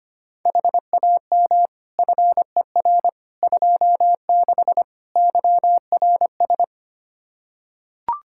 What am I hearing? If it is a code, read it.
Morse code, heard as a keyed tone, about 25 wpm, 700 Hz. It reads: HAM FER 26 YRS
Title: Practice QSO - Exchange 3 - More Info - 25wpm